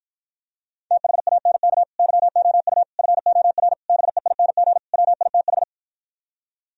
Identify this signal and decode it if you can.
Morse code, heard as a keyed tone, about 52 wpm, 700 Hz. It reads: THANK YOU FOR BEING WITH